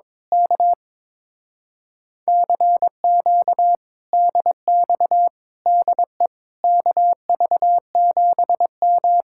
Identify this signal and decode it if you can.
Morse code, heard as a keyed tone, about 22 wpm, 700 Hz. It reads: K CQ DX DE KV7M
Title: Speed-Racing Common QSO Phrases - 22wpm